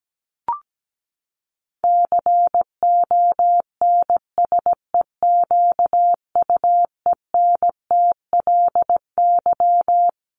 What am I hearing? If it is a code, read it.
Morse code, heard as a keyed tone, about 17 wpm, 700 Hz. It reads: CONSEQUENTLY
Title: LY Suffix - Top 100 Words with Base Words 17wpm